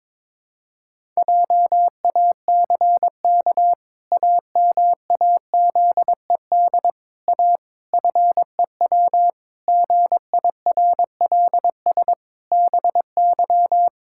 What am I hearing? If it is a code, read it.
Morse code, heard as a keyed tone, about 22 wpm, 700 Hz. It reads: JACK AMAZED A FEW GIRLS BY
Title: Pangram Challenge v2 - 22wpm